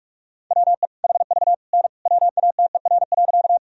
Morse code, heard as a keyed tone, about 46 wpm, 700 Hz. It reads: WE HV N WATER.
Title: Sentences from Top 100 Words and Abbreviations 45wpm